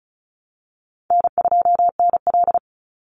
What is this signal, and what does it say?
Morse code, heard as a keyed tone, about 34 wpm, 700 Hz. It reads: N2NL
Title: Call Signs Repeated 2x - 35wpm